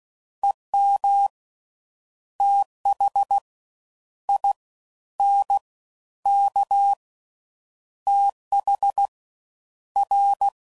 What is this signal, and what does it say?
Morse code, heard as a keyed tone, about 16 wpm, 800 Hz. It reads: EM TH I N K TH R